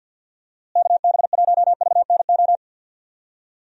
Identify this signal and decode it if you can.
Morse code, heard as a keyed tone, about 50 wpm, 700 Hz. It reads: KB1VNO
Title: USA Call Signs - 6 Characters - Repeated 3x - 50wpm